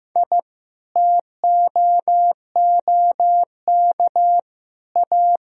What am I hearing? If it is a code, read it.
Morse code, heard as a keyed tone, about 15 wpm, 700 Hz. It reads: I TOOK A
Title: Sentences from Top 1000 Words - Review Part 4 - 15wpm